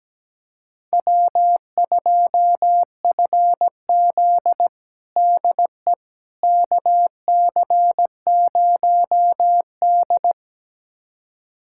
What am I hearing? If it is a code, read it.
Morse code, heard as a keyed tone, about 17 wpm, 700 Hz. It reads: W2FZ DE KC0D
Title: USA Callsign DE Callsign 17wpm